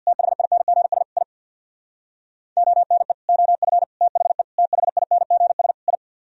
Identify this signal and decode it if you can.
Morse code, heard as a keyed tone, about 49 wpm, 700 Hz. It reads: THINGS I ONE OF THE THINGS I
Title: Speed-Racing - Sets of 5 Words 50wpm